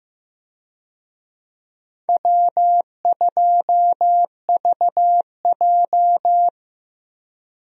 Morse code, heard as a keyed tone, about 15 wpm, 700 Hz. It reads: W2VJ
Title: USA Call Signs - 4 Characters - 15wpm